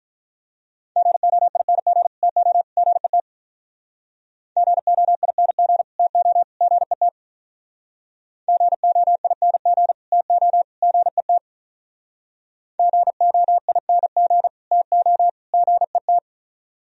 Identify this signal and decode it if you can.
Morse code, heard as a keyed tone, about 46 wpm, 700 Hz. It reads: GOING TO GET GOING TO GET GOING TO GET GOING TO GET